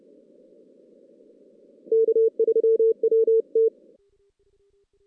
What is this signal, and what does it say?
Morse code, heard as a keyed tone, about 30 wpm, 440 Hz. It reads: K3WT
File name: 2016 JAN CW NAQP / 2016 Jan naqp k5ot - 0-5